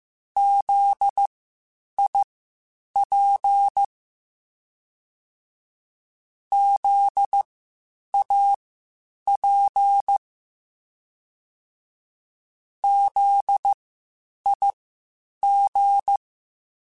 Morse code, heard as a keyed tone, about 15 wpm, spaced as for 8 wpm, 800 Hz. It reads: ZIP ZAP ZIG
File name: Spaced Alphas